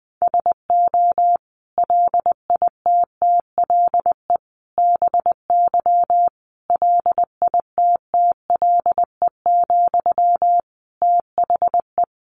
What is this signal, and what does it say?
Morse code, heard as a keyed tone, about 20 wpm, 700 Hz. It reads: SO LITTLE BY LITTLE, THE
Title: Sentences from Top 1500 Words 20wpm